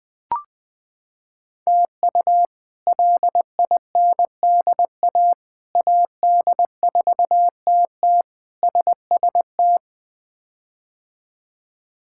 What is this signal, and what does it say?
Morse code, heard as a keyed tone, about 20 wpm, 700 Hz. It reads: TU LINDA AD4TT SST
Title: K1USN SST QSO Practice - 20wpm